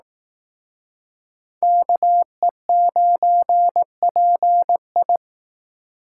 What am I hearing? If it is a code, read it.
Morse code, heard as a keyed tone, about 18 wpm, 700 Hz. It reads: KE9PI
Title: Speed Racing - USA 2x2 Call Signs 15wpm